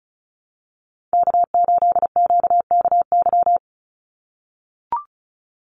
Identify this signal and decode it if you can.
Morse code, heard as a keyed tone, about 34 wpm, 700 Hz. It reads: K8QKY
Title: Call Signs - 35wpm